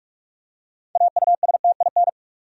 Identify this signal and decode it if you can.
Morse code, heard as a keyed tone, about 46 wpm, 700 Hz. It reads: AUSTIN
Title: USA Common Names - 45wpm